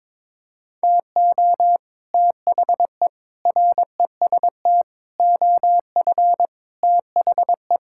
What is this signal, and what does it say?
Morse code, heard as a keyed tone, about 22 wpm, 700 Hz. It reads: TO THE REST OF THE
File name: practice / 22wpm / Sets of 6 Words - 22wpm